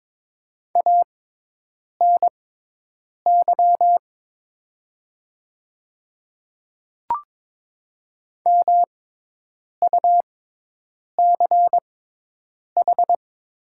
Morse code, heard as a keyed tone, about 22 wpm, spaced as for 8 wpm, 700 Hz. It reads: ANY MUCH
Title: 6x Character Spacing - Top 300 Words - 22wpm